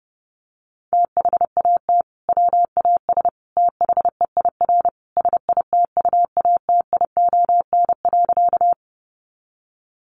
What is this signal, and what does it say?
Morse code, heard as a keyed tone, about 30 wpm, 700 Hz. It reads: THAT WAS THEIR SITUATION.
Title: Sentences from Top 1200 Words 30wpm